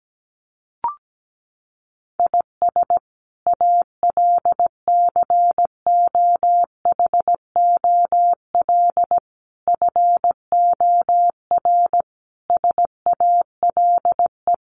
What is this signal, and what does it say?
Morse code, heard as a keyed tone, about 17 wpm, 700 Hz. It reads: IS ALCOHOL FOR SALE
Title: Sentences from Top 1400 Words 17wpm